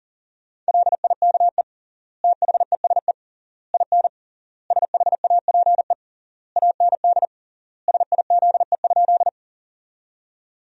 Morse code, heard as a keyed tone, about 40 wpm, 700 Hz. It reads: LIKE THESE IN SHAPE AND SIZE?
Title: 3x Word-Spacing - Sentences from Top 500 Words - 40wpm